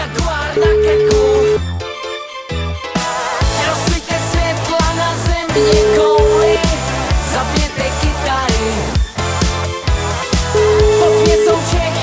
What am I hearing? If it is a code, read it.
Ringback tone (Europe, 425 Hz), with noise.